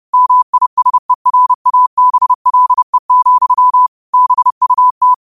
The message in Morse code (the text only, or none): MISERABLE, BUT